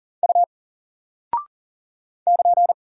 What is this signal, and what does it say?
Morse code, heard as a keyed tone, about 40 wpm, 700 Hz. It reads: U <KN>
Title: Single Character (Letter:Number:Punctuation:Prosign) Alt 40wpm